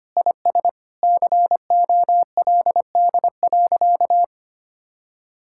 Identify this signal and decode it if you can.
Morse code, heard as a keyed tone, about 25 wpm, 700 Hz. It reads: IS COLD.
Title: Sentences from Top 1600 Words 25wpm